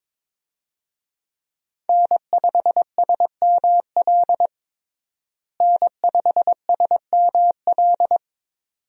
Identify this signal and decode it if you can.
Morse code, heard as a keyed tone, about 22 wpm, 700 Hz. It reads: N5SML N5SML
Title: USA Call Signs - 5 Characters - Repeated 2x - 22wpm